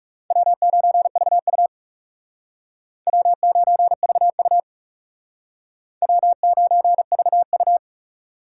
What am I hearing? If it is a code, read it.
Morse code, heard as a keyed tone, about 40 wpm, 700 Hz. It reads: W9VU W9VU W9VU